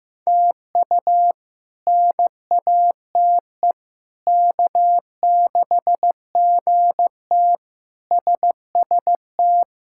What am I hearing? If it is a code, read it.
Morse code, heard as a keyed tone, about 15 wpm, 700 Hz. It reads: TU NATE K6GT SST
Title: K1USN SST QSO Practice - 15wpm